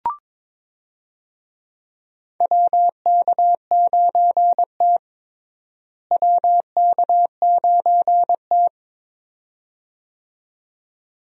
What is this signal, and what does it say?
Morse code, heard as a keyed tone, about 21 wpm, 700 Hz. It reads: WK9T WK9T
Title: USA Call Signs - 4 Characters - Repeated 2x - 22wpm